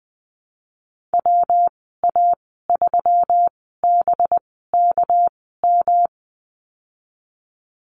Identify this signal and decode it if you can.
Morse code, heard as a keyed tone, about 20 wpm, spaced as for 15 wpm, 700 Hz. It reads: WA3BKM
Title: 231 Lesson 37g - Letter x - Call Signs 20wpm